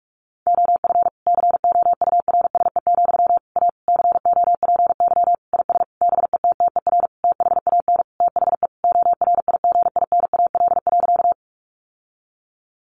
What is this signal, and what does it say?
Morse code, heard as a keyed tone, about 44 wpm, 700 Hz. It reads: OF COURSE, A COPY IS BETTER THAN THE ORIGINAL.
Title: Sentences from Top 1400 Words 45wpm